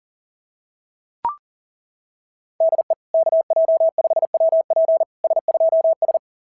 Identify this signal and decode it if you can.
Morse code, heard as a keyed tone, about 38 wpm, 690 Hz. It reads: DE KJ5WP S2S
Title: SOTA QSO Practice 40wpm